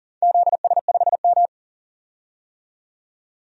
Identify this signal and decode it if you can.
Morse code, heard as a keyed tone, about 40 wpm, 700 Hz. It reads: ZS5M
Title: DXCC-100 Call Signs - 4 Characters - 40wpm